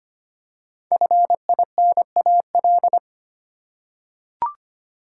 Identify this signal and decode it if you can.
Morse code, heard as a keyed tone, about 25 wpm, 700 Hz. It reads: FINAL
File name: 5 Letter Words - First 3 Letters Given - 25wpm